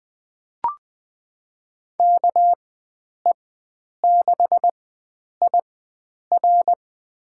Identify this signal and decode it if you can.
Morse code, heard as a keyed tone, about 21 wpm, spaced as for 9 wpm, 700 Hz. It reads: KE6IR
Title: USA Call Signs - 5 Characters - 4x Character Spacing - 20wpm